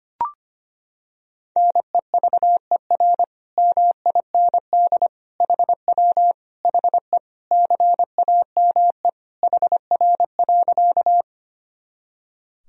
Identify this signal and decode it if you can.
Morse code, heard as a keyed tone, about 25 wpm, 700 Hz. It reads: NEVER MIND HW HE CAME HR.
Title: Sentences from Top 500 Words and Abbreviations 25wpm